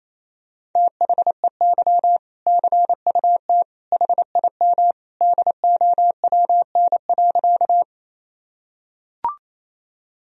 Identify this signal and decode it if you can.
Morse code, heard as a keyed tone, about 28 wpm, 700 Hz. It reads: THEY CUT HIM DOWN.
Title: Sentences from Top 1000 Words - Review Part 1 - 28wpm